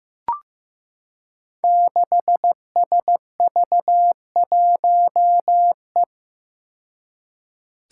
Morse code, heard as a keyed tone, about 15 wpm, 700 Hz. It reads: 6SV1E
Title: Five Challenging Letter-Number Combinations 15wpm